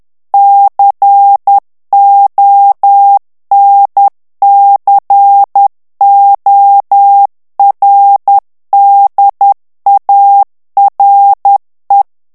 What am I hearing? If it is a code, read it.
Morse code, heard as a keyed tone, about 10 wpm, 800 Hz. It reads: CONCORDARE
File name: morse mp3 / ITALIEN texte clair 1 vitesse 10 mots minute